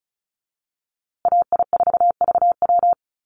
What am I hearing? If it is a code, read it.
Morse code, heard as a keyed tone, about 34 wpm, 700 Hz. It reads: AI4VW
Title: USA Call Signs Repeated 3x 35wpm